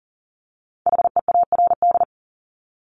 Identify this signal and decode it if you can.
Morse code, heard as a keyed tone, about 39 wpm, 700 Hz. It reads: HEARD
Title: Top 500 Words 40wpm